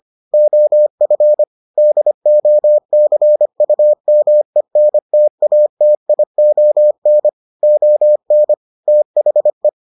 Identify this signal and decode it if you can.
Morse code, heard as a keyed tone, about 25 wpm, 600 Hz. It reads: OF DOCUMENTATION ON THE